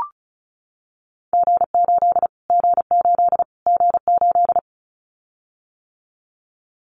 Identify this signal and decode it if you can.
Morse code, heard as a keyed tone, about 34 wpm, 700 Hz. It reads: G8 G8 G8
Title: Callsign Prefixes for Top 25 DXCC - Repeated 3x - 35wpm